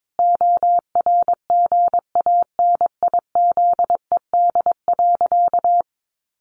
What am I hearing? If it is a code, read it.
Morse code, heard as a keyed tone, about 22 wpm, 700 Hz. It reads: ORGANIZED.
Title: Sentences from Top 1300 Words 22wpm